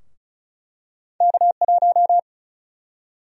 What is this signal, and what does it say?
Morse code, heard as a keyed tone, about 34 wpm, 700 Hz. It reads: K1
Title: USA Call Sign Prefixes - 35wpm